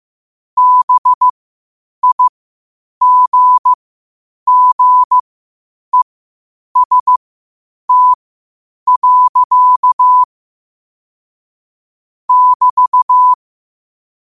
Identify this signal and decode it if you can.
Morse code, heard as a keyed tone, about 15 wpm, spaced as for 8 wpm, 1000 Hz. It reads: BIGGEST. <BT>
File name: qotdfiles / 8wpm